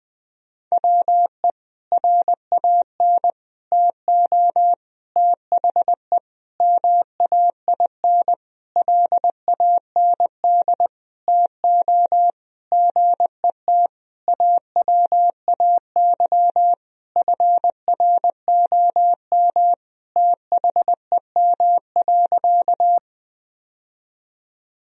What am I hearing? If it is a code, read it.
Morse code, heard as a keyed tone, about 21 wpm, 700 Hz. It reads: WE RAN TO THE MAIN LAND TO GET AWAY FROM THEM.
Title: Sentences from Top 1000 Words - Review Part 2 - 20wpm